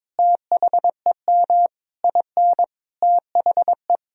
Morse code, heard as a keyed tone, about 22 wpm, 700 Hz. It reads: THEM IN THE
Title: Sentences from Top 1000 Words - Review Part 2 - 22wpm